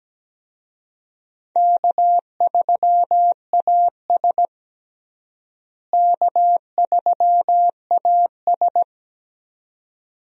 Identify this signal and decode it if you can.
Morse code, heard as a keyed tone, about 17 wpm, 700 Hz. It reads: K3AS K3AS
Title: USA Call Signs - 4 Characters - Repeated 2x - 17wpm